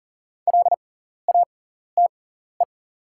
Morse code, heard as a keyed tone, about 40 wpm, spaced as for 14 wpm, 700 Hz. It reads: LATE
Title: 6x Character Spacing - Top 500 Words - 40wpm